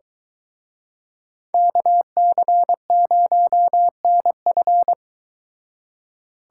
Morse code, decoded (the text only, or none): KC0NF